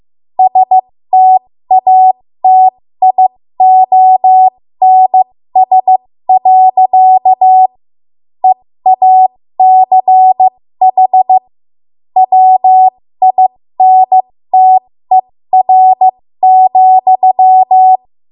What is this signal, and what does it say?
Morse code, heard as a keyed tone, about 15 wpm, spaced as for 13 wpm, 750 Hz. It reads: STATIONS. EACH WINTER,
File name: Morse / 13WPM